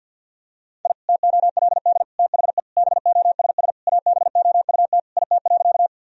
Morse code, heard as a keyed tone, about 50 wpm, 700 Hz. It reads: I TOLD THE BOSS ABOUT IT.
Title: Sentences from Top 1000 Words - Review Part 4 - 50wpm